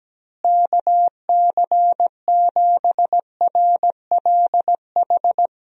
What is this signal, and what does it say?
Morse code, heard as a keyed tone, about 17 wpm, 700 Hz. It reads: KC7RLH
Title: USA Call Signs - 6 Characters - Repeated 3x - 17wpm